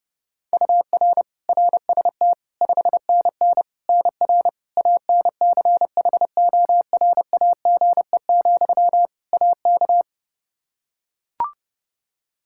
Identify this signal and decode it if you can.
Morse code, heard as a keyed tone, about 30 wpm, 700 Hz. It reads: UR RST 5NN NR ANCHORAGE, AK
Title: Practice QSO - Exchange 1 - Beginning - 30wpm